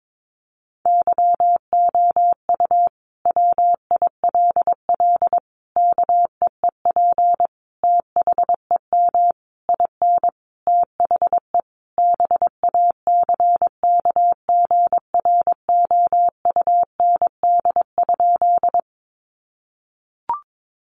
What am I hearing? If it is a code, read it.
Morse code, heard as a keyed tone, about 22 wpm, 700 Hz. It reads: YOU WILL KEEP THEM IN THE BACKGROUND?
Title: Sentences from Top 1300 Words 22wpm